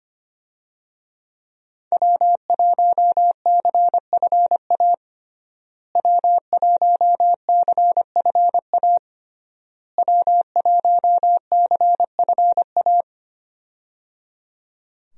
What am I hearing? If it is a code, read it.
Morse code, heard as a keyed tone, about 25 wpm, 700 Hz. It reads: W1CFA W1CFA W1CFA
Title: USA Call Signs - 5 Characters - Repeated 3x - 25wpm